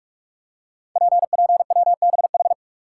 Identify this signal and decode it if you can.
Morse code, heard as a keyed tone, about 44 wpm, 700 Hz. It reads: PPWBH
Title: Code Groups - 5 Letters - 45wpm